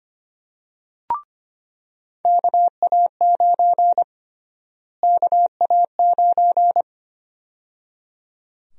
Morse code, heard as a keyed tone, about 25 wpm, 700 Hz. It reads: KA9 KA9
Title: USA Call Sign Prefixes - Repeated 2x - 25wpm